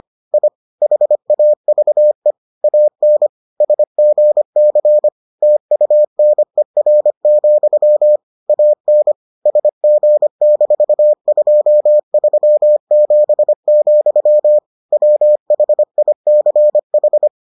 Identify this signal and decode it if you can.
Morse code, heard as a keyed tone, about 25 wpm, 600 Hz. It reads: I HAVE AN SGC TUNER, AN SG-237, WHICH